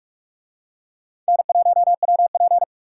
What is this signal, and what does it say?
Morse code, heard as a keyed tone, about 44 wpm, 700 Hz. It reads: N1WP